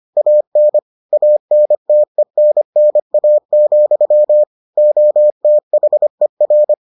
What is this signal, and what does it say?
Morse code, heard as a keyed tone, about 25 wpm, 600 Hz. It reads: AN ANTENNA, OTHER